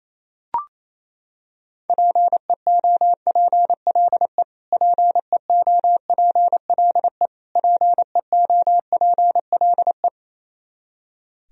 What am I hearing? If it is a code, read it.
Morse code, heard as a keyed tone, about 27 wpm, 700 Hz. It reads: PEOPLE PEOPLE PEOPLE
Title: Top 100 Words Repeated 28wpm